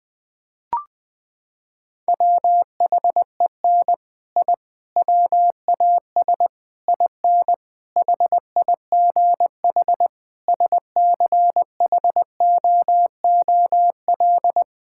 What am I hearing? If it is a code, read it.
Morse code, heard as a keyed tone, about 20 wpm, 700 Hz. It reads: WHEN I WAS IN HIGH SCHOOL
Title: Sentences with 6-Word Phrases - 20wpm